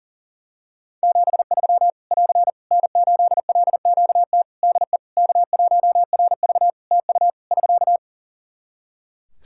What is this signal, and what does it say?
Morse code, heard as a keyed tone, about 40 wpm, 700 Hz. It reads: 73 <AR> N8LQT DE K1RV TU <SK>
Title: K1USN SST QSO Practice 40wpm